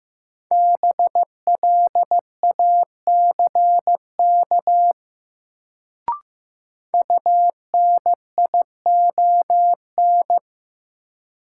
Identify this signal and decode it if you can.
Morse code, heard as a keyed tone, about 15 wpm, 700 Hz. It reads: BLACK UNION
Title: Sets of 5 Words Encore - New Word Drill - 15wpm